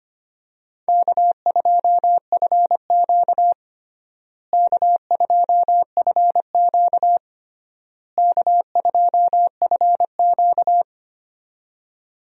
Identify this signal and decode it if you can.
Morse code, heard as a keyed tone, about 25 wpm, 700 Hz. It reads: K2FQ K2FQ K2FQ
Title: USA Call Signs - 4 Characters - Repeated 3x - 25wpm